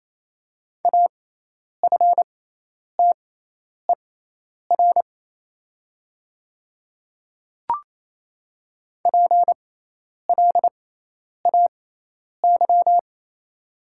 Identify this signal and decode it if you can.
Morse code, heard as a keyed tone, about 27 wpm, spaced as for 10 wpm, 690 Hz. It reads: AFTER PLAY
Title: 6x Character Spacing - Top 300 Words - 28wpm